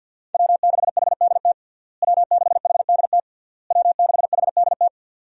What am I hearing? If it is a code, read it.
Morse code, heard as a keyed tone, about 47 wpm, 700 Hz. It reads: W6HDT W6HDT W6HDT
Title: USA Call Signs - 5 Characters - Repeated 3x - 50wpm